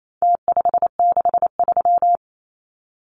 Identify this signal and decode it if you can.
Morse code, heard as a keyed tone, about 28 wpm, 700 Hz. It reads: T563
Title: Numbers - Contest Style - 4-Digit 28wpm